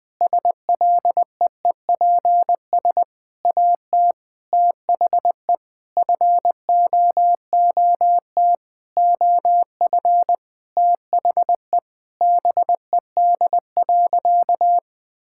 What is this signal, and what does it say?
Morse code, heard as a keyed tone, about 20 wpm, 700 Hz. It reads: SLEEPS AT THE FOOT OF THE BED.
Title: Sentences with 5-Word Phrases - 20wpm